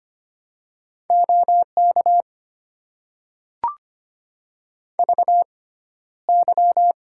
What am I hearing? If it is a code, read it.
Morse code, heard as a keyed tone, about 25 wpm, spaced as for 13 wpm, 700 Hz. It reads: OK VY
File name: Top 90 QSO Elements - 6x Character Spacing - 25wpm